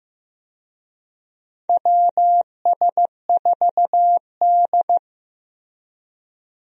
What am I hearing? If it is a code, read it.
Morse code, heard as a keyed tone, about 15 wpm, 700 Hz. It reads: WS4D
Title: USA Call Signs - 4 Characters - Repeated 2x - 15wpm